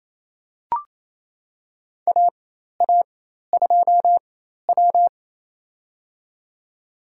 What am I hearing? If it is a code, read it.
Morse code, heard as a keyed tone, about 28 wpm, spaced as for 13 wpm, 700 Hz. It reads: AA2W
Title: USA Call Signs - 4 Characters - 4x Character Spacing - 28wpm